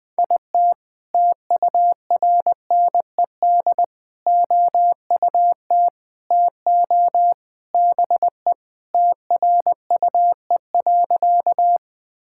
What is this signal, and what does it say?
Morse code, heard as a keyed tone, about 20 wpm, 700 Hz. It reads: IT TURNED OUT TO BE TRUE.
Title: Sentences with 3-Word Phrases - 20wpm